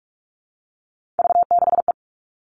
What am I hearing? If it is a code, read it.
Morse code, heard as a keyed tone, about 44 wpm, 700 Hz. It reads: V6E
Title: All Cut Numbers - 3-Digit 45wpm